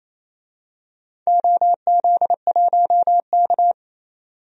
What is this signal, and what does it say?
Morse code, heard as a keyed tone, about 28 wpm, 700 Hz. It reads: OZ1K
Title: DXCC-100 Call Signs - 4 Characters - 28wpm